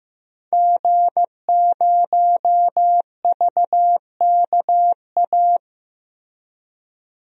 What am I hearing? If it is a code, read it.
Morse code, heard as a keyed tone, about 15 wpm, 700 Hz. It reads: G0VKA
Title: DXCC-100 Call Signs - 5 Characters - 15wpm